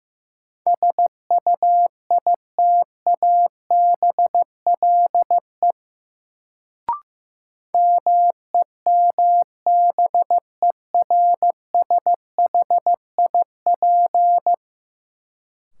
Morse code, heard as a keyed tone, about 15 wpm, 700 Hz. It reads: SUITABLE MEMBERSHIP
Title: Sentences from Top 1400 Words - New Word Drill - 15wpm